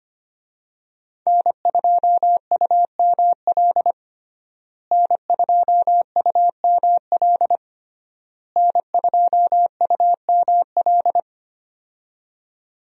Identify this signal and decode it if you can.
Morse code, heard as a keyed tone, about 25 wpm, 700 Hz. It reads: N2UML N2UML N2UML
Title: USA Call Signs - 5 Characters - Repeated 3x - 25wpm